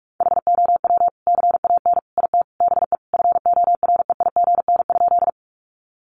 Morse code, heard as a keyed tone, about 44 wpm, 700 Hz. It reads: HOW CAN IT BE FOREIGN?
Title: Sentences from Top 1400 Words 45wpm